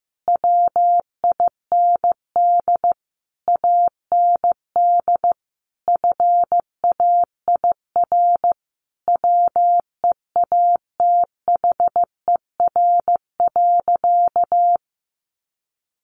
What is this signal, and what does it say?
Morse code, heard as a keyed tone, about 15 wpm, 700 Hz. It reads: WIND AND FAIR WEATHER.
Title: Sentences from Top 1300 Words 15wpm